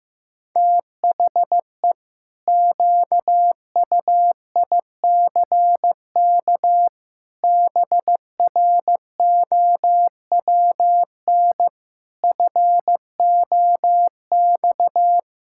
Morse code, heard as a keyed tone, about 15 wpm, 700 Hz. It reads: THE QUICK BROWN FOX
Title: Pangram Challenge v2 - 15wpm